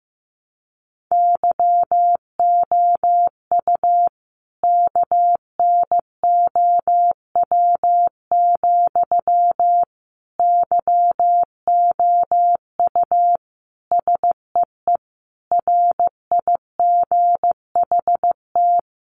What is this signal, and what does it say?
Morse code, heard as a keyed tone, about 15 wpm, 700 Hz. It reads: YOU KNOW, YOU SEE RIGHT
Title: Sentences from Top 200 Words 15wpm